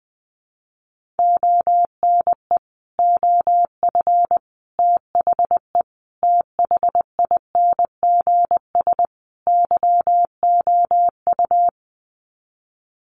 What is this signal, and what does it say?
Morse code, heard as a keyed tone, about 20 wpm, 700 Hz. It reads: ONE OF THE THINGS YOU
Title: Sets of 5 Words Encore2 20wpm